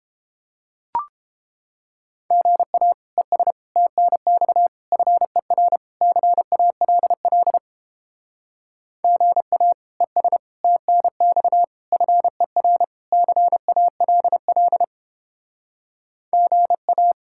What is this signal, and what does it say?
Morse code, heard as a keyed tone, about 29 wpm, 700 Hz. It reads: GA ES TNX FER CALL GA ES TNX FER CALL GA